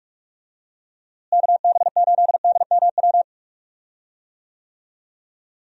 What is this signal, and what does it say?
Morse code, heard as a keyed tone, about 43 wpm, 700 Hz. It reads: KB8DMW
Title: USA Call Signs - 6 Characters - Repeated 2x - 45wpm